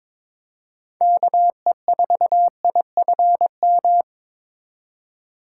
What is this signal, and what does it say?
Morse code, heard as a keyed tone, about 27 wpm, 700 Hz. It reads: KE4IFM
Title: Speed Racing - USA 2x3 Call Signs 22wpm